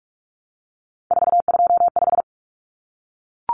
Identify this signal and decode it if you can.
Morse code, heard as a keyed tone, about 44 wpm, 700 Hz. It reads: 425
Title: Numbers - 3-Digit 45wpm